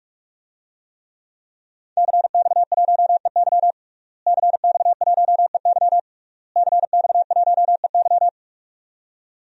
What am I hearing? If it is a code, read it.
Morse code, heard as a keyed tone, about 46 wpm, 700 Hz. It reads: CX1EY CX1EY CX1EY
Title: DXCC-100 Call Signs - 5 Characters - Repeated 3x - 45wpm